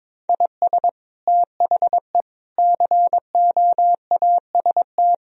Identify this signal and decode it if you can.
Morse code, heard as a keyed tone, about 22 wpm, 700 Hz. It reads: IS THE COAST
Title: Sentences from Top 1000 Words - Review Part 4 - 22wpm